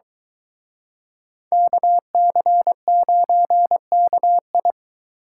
Morse code, heard as a keyed tone, about 23 wpm, 700 Hz. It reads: KC9KI